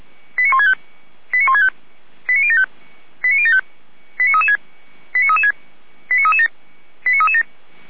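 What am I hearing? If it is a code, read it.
Five-tone selective call (ZVEI), heard as quick runs of five tones, tones of about 70 ms. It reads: 89165 89165 89075 89075 89336 89336 89337 89337
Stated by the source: ZVEI five-tone selective call received on an SDR.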